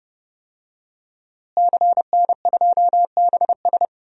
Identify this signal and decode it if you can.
Morse code, heard as a keyed tone, about 31 wpm, 700 Hz. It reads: CN2BS
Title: DXCC-100 Call Signs - 5 Characters - Repeated 2x - 30wpm